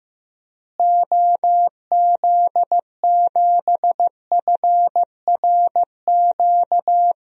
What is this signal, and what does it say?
Morse code, heard as a keyed tone, about 15 wpm, 700 Hz. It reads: OZ7FRQ
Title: DXCC-100 Call Signs - 6 Characters - Repeated 3x - 15wpm